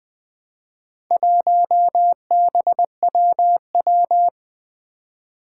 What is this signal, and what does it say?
Morse code, heard as a keyed tone, about 20 wpm, 700 Hz. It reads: 1BWW
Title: Four Challenging Letter-Number Combinations 20wpm